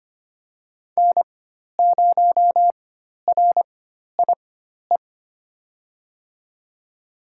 Morse code, heard as a keyed tone, about 25 wpm, spaced as for 12 wpm, 700 Hz. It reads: N0RIE
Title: USA Call Signs - 4x Character Spacing - 25wpm